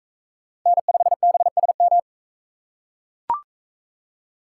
Morse code, heard as a keyed tone, about 41 wpm, 700 Hz. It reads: N5BSM